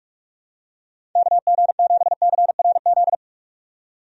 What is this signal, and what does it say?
Morse code, heard as a keyed tone, about 46 wpm, 700 Hz. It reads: KG7CRZ
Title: USA Call Signs - 6 Characters - 45wpm